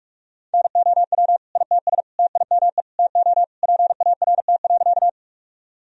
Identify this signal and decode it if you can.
Morse code, heard as a keyed tone, about 44 wpm, 700 Hz. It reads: NOW ITS TIME TO PART.
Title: Sentences from Top 1000 Words - Review Part 1 - 45wpm